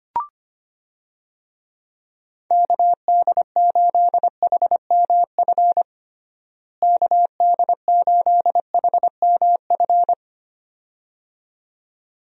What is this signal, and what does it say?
Morse code, heard as a keyed tone, about 25 wpm, 700 Hz. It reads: KD8HMF KD8HMF
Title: USA Call Signs - 6 Characters - Repeated 2x - 25wpm